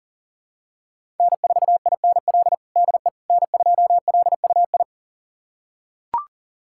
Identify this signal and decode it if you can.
Morse code, heard as a keyed tone, about 40 wpm, 700 Hz. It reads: N4INL DE N2LUI
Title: USA Callsign DE Callsign - 5 Characters - 40wpm